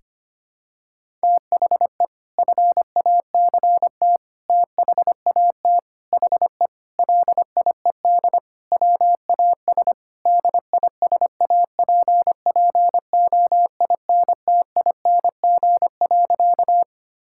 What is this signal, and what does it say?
Morse code, heard as a keyed tone, about 25 wpm, 700 Hz. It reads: THE FACT THAT HE LIED WAS DISAPPOINTING.
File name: practice / 25wpm / Sentences with 4-Word Phrases - 25wpm